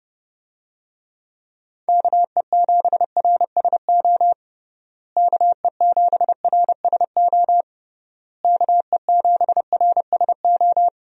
Morse code, heard as a keyed tone, about 30 wpm, 700 Hz. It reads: KE7RSO KE7RSO KE7RSO
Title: USA Call Signs - 6 Characters - Repeated 3x - 30wpm